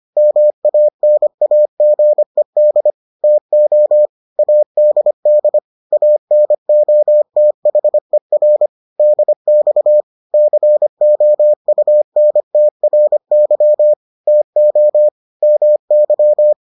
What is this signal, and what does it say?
Morse code, heard as a keyed tone, about 25 wpm, 600 Hz. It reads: MANAGED TO ADD ANOTHER DX COUNTRY TO MY